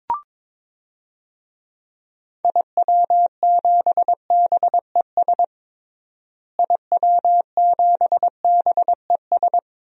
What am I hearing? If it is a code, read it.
Morse code, heard as a keyed tone, about 22 wpm, 700 Hz. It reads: IW7BES IW7BES
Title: DXCC-100 Call Signs - 6 Characters - Repeated 3x - 22wpm